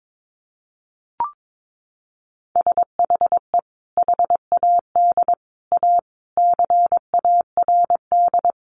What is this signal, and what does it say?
Morse code, heard as a keyed tone, about 22 wpm, 700 Hz. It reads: SHE HAD A CARD
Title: Sentences from Top 600 Words 22wpm